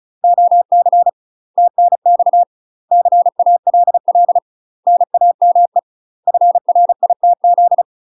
Morse code, heard as a keyed tone, about 34 wpm, 700 Hz. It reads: OC TNX CALL NAME FRITZ
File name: UK MASTER QSOs TRAINING / 35WPM / UK MASTER QSOs 0015